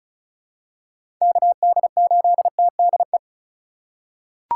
Morse code, heard as a keyed tone, about 36 wpm, 700 Hz. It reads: KD8TDE
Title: USA Call Signs - 6 Characters - 2x Character Spacing - 35wpm